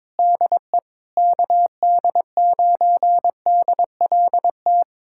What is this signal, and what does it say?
Morse code, heard as a keyed tone, about 22 wpm, 700 Hz. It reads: DE KD9DLT
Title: USA Callsign DE Callsign 22wpm